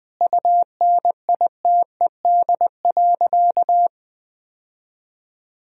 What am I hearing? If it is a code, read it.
Morse code, heard as a keyed tone, about 20 wpm, 700 Hz. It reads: UNITED.
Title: Sentences from Top 1000 Words - Review Part 4 - 20wpm